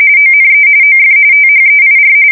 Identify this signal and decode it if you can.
RTTY, about 45.45 baud. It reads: ONE OF THEIR
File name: fsk170 / battleships